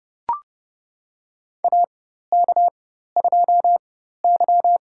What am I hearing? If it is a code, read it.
Morse code, heard as a keyed tone, about 29 wpm, spaced as for 14 wpm, 700 Hz. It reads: AK2Y
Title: USA Call Signs - 4 Characters - 4x Character Spacing - 30wpm